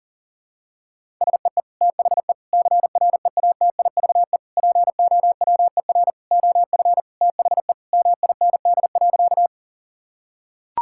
Morse code, heard as a keyed tone, about 40 wpm, 700 Hz. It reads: SEE THE CREATIVE POWER OF THE MIND.
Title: Sentences from Top 1000 Words - Review Part 5 - 40wpm